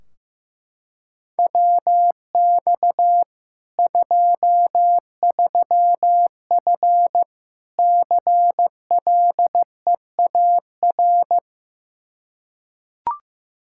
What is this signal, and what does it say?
Morse code, heard as a keyed tone, about 16 wpm, 700 Hz. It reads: WX 23F CLEAR
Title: K1USN SST QSO Practice 15wpm